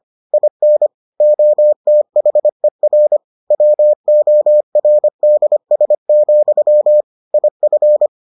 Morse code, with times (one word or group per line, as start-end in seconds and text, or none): 0.3-0.9 IN
1.2-3.2 OTHER
3.5-7.0 WORDS,
7.3-8.1 IF